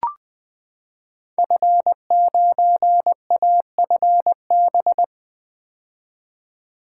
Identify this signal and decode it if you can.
Morse code, heard as a keyed tone, about 20 wpm, 700 Hz. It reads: F9AFB
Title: Code Groups - 5 Letter:Numbers - 20wpm